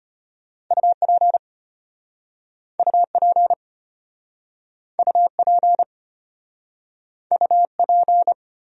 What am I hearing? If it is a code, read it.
Morse code, heard as a keyed tone, about 33 wpm, 700 Hz. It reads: UP UP UP UP